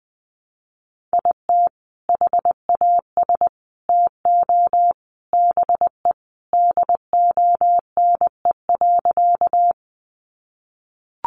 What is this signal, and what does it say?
Morse code, heard as a keyed tone, about 20 wpm, 700 Hz. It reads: IT HAS TO BE DONE.
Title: Sentences from Top 500 Words 20wpm